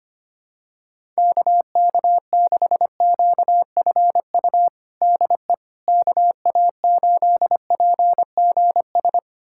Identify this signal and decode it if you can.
Morse code, heard as a keyed tone, about 25 wpm, 700 Hz. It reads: KK6QFU DE KA8PGS
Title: USA Callsign DE Callsign - 6 Characters - 25wpm